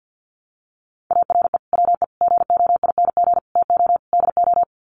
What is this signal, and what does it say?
Morse code, heard as a keyed tone, about 50 wpm, 700 Hz. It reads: ARE WE GOING TO DO
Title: Sets of 5 Words 50wpm